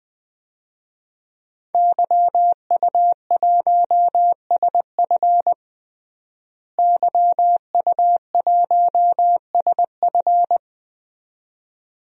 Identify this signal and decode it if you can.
Morse code, heard as a keyed tone, about 20 wpm, 700 Hz. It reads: YU1SF YU1SF
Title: DXCC-100 Call Signs - 5 Characters - Repeated 2x - 20wpm